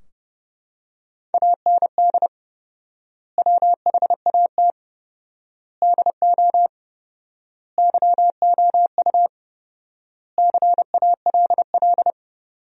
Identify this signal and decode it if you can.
Morse code, heard as a keyed tone, about 30 wpm, 700 Hz. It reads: AND WHAT DO YOU CALL
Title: 4x Word-Spacing - Sentences from Top 100 Words - 30wpm